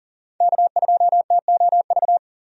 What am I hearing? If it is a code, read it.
Morse code, heard as a keyed tone, about 37 wpm, 700 Hz. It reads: K2TOV
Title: USA Call Signs - 5 Characters - 40wpm